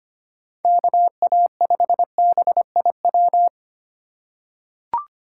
Morse code, heard as a keyed tone, about 25 wpm, 700 Hz. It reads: KA5BIW
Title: USA Call Signs - 6 Characters - 4x Character Spacing - 25wpm